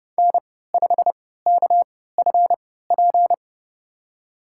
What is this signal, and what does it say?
Morse code, heard as a keyed tone, about 30 wpm, spaced as for 17 wpm, 700 Hz. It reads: N5KFP
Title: USA Call Signs - 5 Characters - 3x Character Spacing - 30wpm